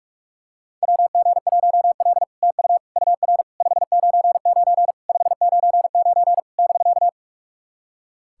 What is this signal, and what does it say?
Morse code, heard as a keyed tone, about 42 wpm, 690 Hz. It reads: WG1L TU UR 599 599 <BK>
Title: SOTA QSO Practice 45wpm